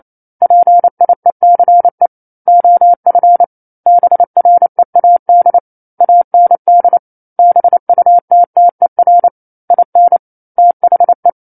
Morse code, heard as a keyed tone, about 28 wpm, 700 Hz. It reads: PIECE OF BREAD AND BUTTER IN THE
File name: 09-Chapter-09 28 WPM